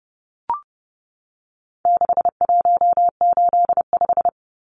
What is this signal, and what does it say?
Morse code, heard as a keyed tone, about 30 wpm, 700 Hz. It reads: 6185
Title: Numbers - 4-Digit 30wpm